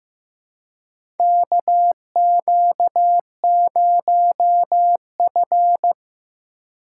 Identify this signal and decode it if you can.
Morse code, heard as a keyed tone, about 15 wpm, 700 Hz. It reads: KQ0F
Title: USA Call Signs - 4 Characters - 4x Character Spacing - 15wpm